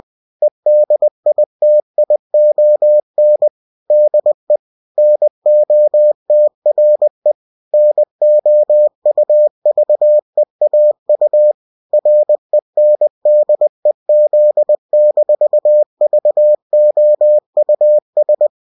Morse code, heard as a keyed tone, about 20 wpm, 600 Hz. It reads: EDITION DE NOTRE NOUVEAU RENDEZ-VOUS